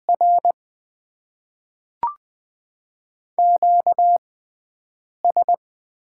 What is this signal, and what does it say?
Morse code, heard as a keyed tone, about 20 wpm, spaced as for 7 wpm, 700 Hz. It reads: R QS
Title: Top 90 QSO Elements - 6x Character Spacing - 20wpm